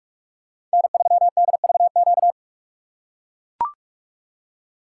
Morse code, heard as a keyed tone, about 44 wpm, 700 Hz. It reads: N3DVQ
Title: USA Call Signs - 3x Character Spacing - 45wpm